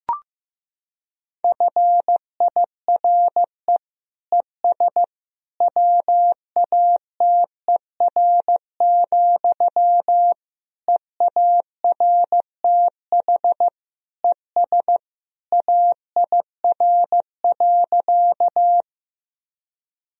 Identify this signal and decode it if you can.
Morse code, heard as a keyed tone, about 15 wpm, 700 Hz. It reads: FIRE ES WATER, EARTH ES AIR.
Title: Sentences from Top 400 Words and Abbreviations 15wpm